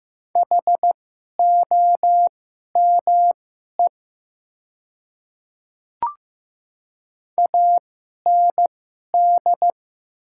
Morse code, heard as a keyed tone, about 15 wpm, spaced as for 11 wpm, 700 Hz. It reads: HOME AND
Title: 2x Character Spacing - Top 200 Words - 15wpm